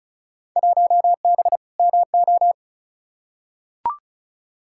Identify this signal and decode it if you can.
Morse code, heard as a keyed tone, about 34 wpm, 700 Hz. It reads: 1B MO
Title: Field Day Exchanges - 35wpm